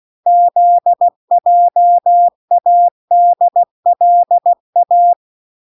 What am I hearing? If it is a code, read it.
Morse code, heard as a keyed tone, about 16 wpm, 700 Hz. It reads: ZJADLA